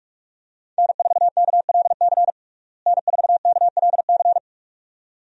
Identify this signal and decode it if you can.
Morse code, heard as a keyed tone, about 44 wpm, 700 Hz. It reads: N4KLC N4KLC
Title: USA Call Signs - 5 Characters - Repeated 2x - 45wpm